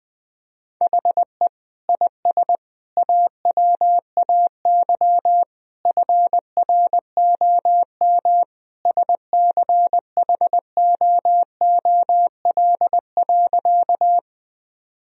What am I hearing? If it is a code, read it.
Morse code, heard as a keyed tone, about 20 wpm, 700 Hz. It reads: HE IS AWAY FROM SCHOOL.
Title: Sentences from Top 1000 Words - Review Part 1 - 20wpm